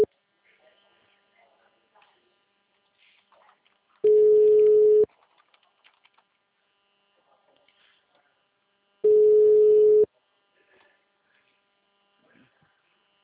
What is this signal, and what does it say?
Ringback tone (Europe, 425 Hz), with speech.